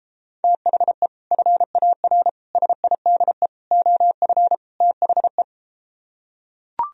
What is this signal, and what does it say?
Morse code, heard as a keyed tone, about 33 wpm, 700 Hz. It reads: THE FAR SIDE OF THE